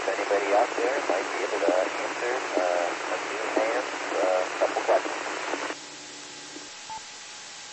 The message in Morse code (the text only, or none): none